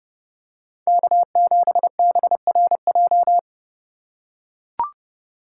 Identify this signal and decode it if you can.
Morse code, heard as a keyed tone, about 31 wpm, 700 Hz. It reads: K7BRJ
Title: USA Call Signs - 5 Characters - 4x Character Spacing - 30wpm